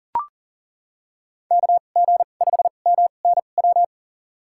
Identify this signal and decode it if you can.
Morse code, heard as a keyed tone, about 40 wpm, spaced as for 29 wpm, 700 Hz. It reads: KG5MNW
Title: USA Call Signs - 6 Characters - 2x Character Spacing - 40wpm